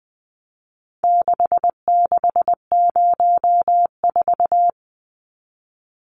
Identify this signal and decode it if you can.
Morse code, heard as a keyed tone, about 19 wpm, 700 Hz. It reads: 6604
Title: Numbers - 4-Digit 20wpm